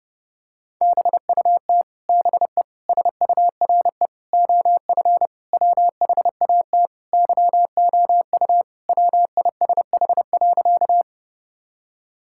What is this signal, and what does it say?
Morse code, heard as a keyed tone, about 32 wpm, 700 Hz. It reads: BUT BE SURE OF WHAT YOU WISH.
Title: Sentences from Top 1600 Words 30wpm